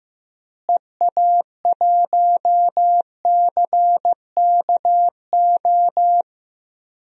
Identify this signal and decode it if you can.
Morse code, heard as a keyed tone, about 15 wpm, 700 Hz. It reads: EA1CKO
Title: DXCC-100 Call Signs - 6 Characters - Repeated 2x - 15wpm